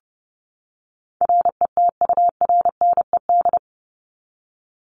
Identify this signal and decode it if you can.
Morse code, heard as a keyed tone, about 30 wpm, 700 Hz. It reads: RETURNED
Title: ED Suffix - Top 100 Words with Base Words 30wpm